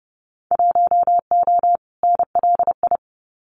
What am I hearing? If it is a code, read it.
Morse code, heard as a keyed tone, about 30 wpm, 700 Hz. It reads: 1O NLI
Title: Winter Field Day Exchanges 30wpm